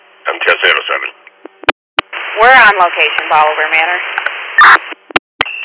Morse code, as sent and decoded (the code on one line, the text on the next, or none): none